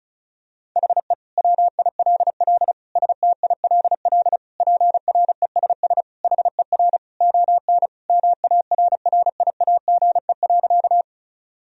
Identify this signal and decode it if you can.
Morse code, heard as a keyed tone, about 34 wpm, 690 Hz. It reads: HE WILL STILL PRESS HER ON MARRIAGE.
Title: Sentences from Top 1000 Words - Review Part 1 - 35wpm